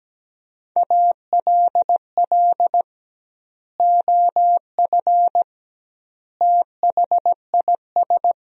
Morse code, heard as a keyed tone, about 17 wpm, 700 Hz. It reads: ALL OF THIS
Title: 2x Word-Spacing - Sentences from Top 500 Words - 17wpm